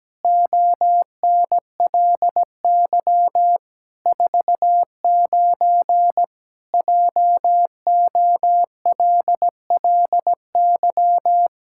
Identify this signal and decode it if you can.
Morse code, heard as a keyed tone, about 17 wpm, 700 Hz. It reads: ONLY 49 JOLLY
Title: Pangram Challenge v2 - 17wpm